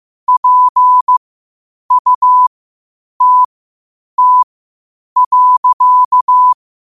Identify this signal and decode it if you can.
Morse code, heard as a keyed tone, about 15 wpm, spaced as for 8 wpm, 1000 Hz. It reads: PUTT.
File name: 8wpm